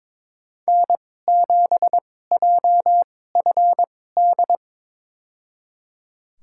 Morse code, heard as a keyed tone, about 22 wpm, spaced as for 16 wpm, 700 Hz. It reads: N7JFD
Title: USA Call Signs - 5 Characters - 2x Character Spacing - 22wpm